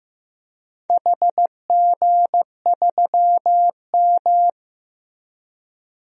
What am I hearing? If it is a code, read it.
Morse code, heard as a keyed tone, about 15 wpm, 700 Hz. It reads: HG3M
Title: DXCC-100 Call Signs - 4 Characters - Repeated 2x - 15wpm